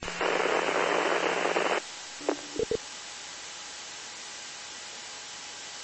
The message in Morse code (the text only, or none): none